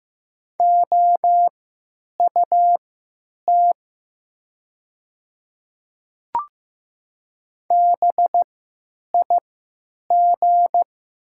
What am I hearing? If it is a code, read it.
Morse code, heard as a keyed tone, about 15 wpm, spaced as for 9 wpm, 700 Hz. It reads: OUT BIG
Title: 3x Character Spacing - Top 200 Words - 15wpm